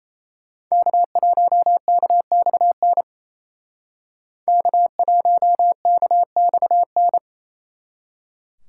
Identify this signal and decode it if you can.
Morse code, heard as a keyed tone, about 36 wpm, 700 Hz. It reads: K1KXN K1KXN